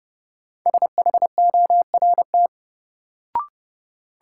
Morse code, heard as a keyed tone, about 30 wpm, 700 Hz. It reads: SHORT
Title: Sets of 5 Words Encore - New Word Drill - 30wpm